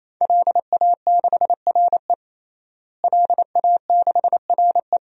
Morse code, heard as a keyed tone, about 28 wpm, 700 Hz. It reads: LA6RE LA6RE
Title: DXCC-100 Call Signs - 5 Characters - Repeated 3x - 28wpm